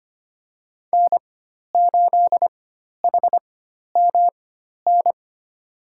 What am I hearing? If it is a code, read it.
Morse code, heard as a keyed tone, about 25 wpm, spaced as for 12 wpm, 700 Hz. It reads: N8HMN
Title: USA Call Signs - 5 Characters - 4x Character Spacing - 25wpm